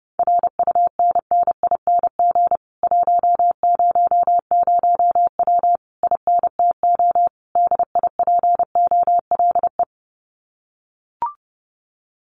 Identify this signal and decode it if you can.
Morse code, heard as a keyed tone, about 30 wpm, 700 Hz. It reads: RUNNING 100W INTO DIPOLE
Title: Common QSO Phrases Encore 30wpm